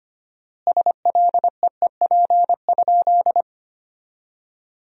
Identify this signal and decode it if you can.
Morse code, heard as a keyed tone, about 25 wpm, 700 Hz. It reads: SLEEP?
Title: 4x Word-Spacing - Sentences from Top 500 Words - 25wpm